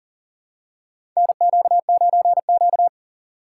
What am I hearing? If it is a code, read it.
Morse code, heard as a keyed tone, about 40 wpm, 700 Hz. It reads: NQ9Q
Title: USA Call Signs - 4 Characters - 4x Character Spacing - 40wpm